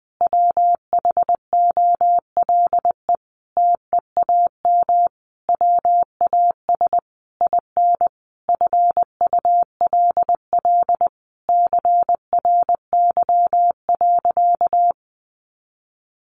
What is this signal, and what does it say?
Morse code, heard as a keyed tone, about 20 wpm, 700 Hz. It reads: WHOLE TEAM WAS IN FULL CRY.
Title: Sentences from Top 1500 Words 20wpm